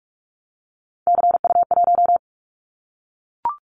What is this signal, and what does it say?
Morse code, heard as a keyed tone, about 44 wpm, 700 Hz. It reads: CU1
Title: Callsign Prefixes for Top 100 DXCC - 45wpm